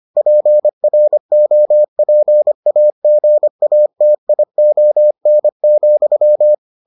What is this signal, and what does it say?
Morse code, heard as a keyed tone, about 25 wpm, 600 Hz. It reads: PROPAGATION,